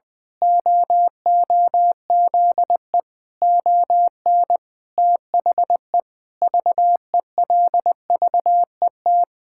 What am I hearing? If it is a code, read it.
Morse code, heard as a keyed tone, about 20 wpm, 700 Hz. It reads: OOZE ON THE VELVET
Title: Pangram Challenge v2 - 20wpm